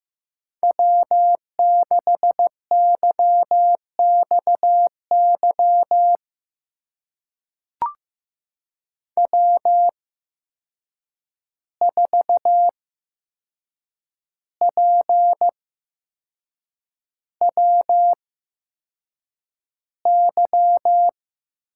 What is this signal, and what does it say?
Morse code, heard as a keyed tone, about 15 wpm, spaced as for 6 wpm, 700 Hz. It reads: W6YXY W4PWY
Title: USA Call Signs - 5 Characters - 8x Character Spacing - 15wpm